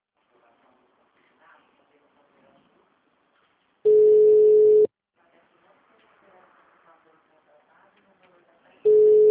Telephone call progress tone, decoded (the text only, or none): ringback tone